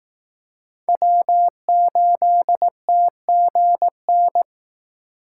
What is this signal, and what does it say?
Morse code, heard as a keyed tone, about 18 wpm, 700 Hz. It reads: W8TGN
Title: Speed Racing - USA 1x3 Call Signs 15wpm